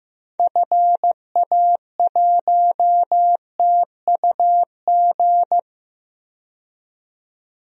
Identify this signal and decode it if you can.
Morse code, heard as a keyed tone, about 15 wpm, 700 Hz. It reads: FA1TUG
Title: DXCC-100 Call Signs - 6 Characters - 15wpm